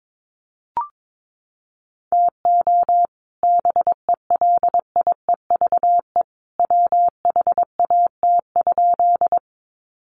Morse code, heard as a keyed tone, about 22 wpm, 700 Hz. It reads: TO BELIEVE WHAT?
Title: Sentences from Top 1000 Words 22wpm